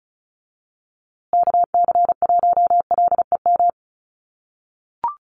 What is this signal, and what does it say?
Morse code, heard as a keyed tone, about 34 wpm, 700 Hz. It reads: KC1LEM
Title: USA Call Signs Repeated 3x 35wpm